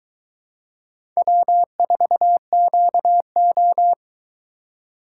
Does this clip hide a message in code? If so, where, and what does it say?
Yes: Morse code from 1.2 to 3.9 s, heard as a keyed tone, about 20 wpm, 700 Hz. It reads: W4QO